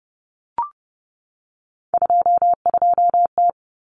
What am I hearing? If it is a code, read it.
Morse code, heard as a keyed tone, about 30 wpm, 700 Hz. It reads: 22T
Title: Common Cut Numbers - 3-Digit 30wpm